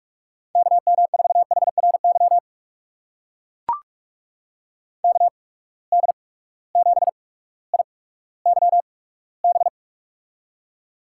Morse code, heard as a keyed tone, about 44 wpm, spaced as for 19 wpm, 700 Hz. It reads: KM4HRY KD7IYB
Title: USA Call Signs - 6 Characters - 8x Character Spacing - 45wpm